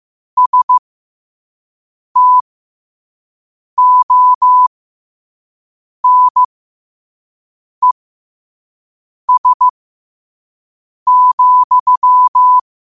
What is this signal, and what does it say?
Morse code, heard as a keyed tone, about 15 wpm, spaced as for 5 wpm, 1000 Hz. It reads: STONES,
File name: qotdfiles / 5wpm